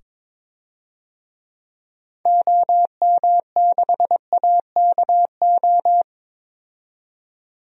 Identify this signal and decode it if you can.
Morse code, heard as a keyed tone, about 21 wpm, 700 Hz. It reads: OM6AKO
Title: DXCC-100 Call Signs - 6 Characters - 22wpm